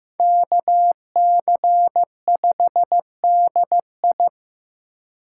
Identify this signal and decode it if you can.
Morse code, heard as a keyed tone, about 18 wpm, 700 Hz. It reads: KC5DI
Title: Speed Racing - USA 2x2 Call Signs 15wpm